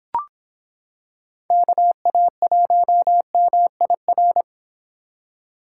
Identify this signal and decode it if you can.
Morse code, heard as a keyed tone, about 26 wpm, 700 Hz. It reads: KA1MIR